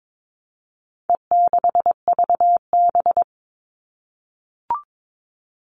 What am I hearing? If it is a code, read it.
Morse code, heard as a keyed tone, about 22 wpm, 700 Hz. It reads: E6VB
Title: All Cut Numbers - 4-Digit 22wpm